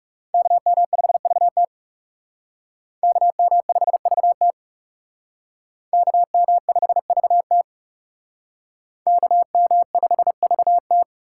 Morse code, heard as a keyed tone, about 38 wpm, 700 Hz. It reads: KM5VT KM5VT KM5VT KM5VT
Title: Speed Racing - USA 2x2 Call Signs 30wpm